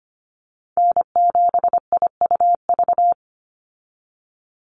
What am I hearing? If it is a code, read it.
Morse code, heard as a keyed tone, about 25 wpm, 700 Hz. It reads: N7IUV
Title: USA Call Signs 25wpm